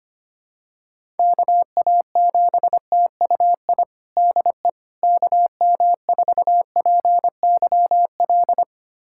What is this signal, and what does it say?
Morse code, heard as a keyed tone, about 25 wpm, 700 Hz. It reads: KA7TUI DE KM4PYL
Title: USA Callsign DE Callsign 25wpm